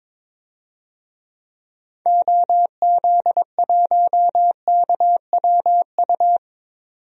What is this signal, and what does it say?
Morse code, heard as a keyed tone, about 22 wpm, 700 Hz. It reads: OZ1KWU
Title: DXCC-100 Call Signs - 6 Characters - 22wpm